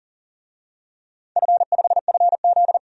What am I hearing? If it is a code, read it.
Morse code, heard as a keyed tone, about 40 wpm, 700 Hz. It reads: F5FZ
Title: DXCC-100 Call Signs - 4 Characters - 40wpm